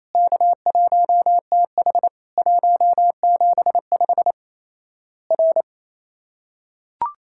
Morse code, heard as a keyed tone, about 27 wpm, 700 Hz. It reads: K1TH 175 R
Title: Worked All Europe DX Contest - QTC Exchange - 28wpm